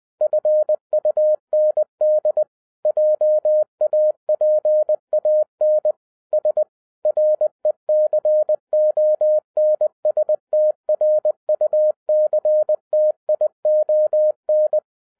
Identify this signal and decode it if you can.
Morse code, heard as a keyed tone, about 20 wpm, 610 Hz. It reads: FUND JAPAN S RECONSTRUCTION